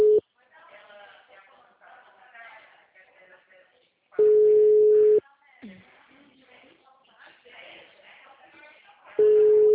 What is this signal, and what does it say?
Ringback tone (Europe, 425 Hz), with speech.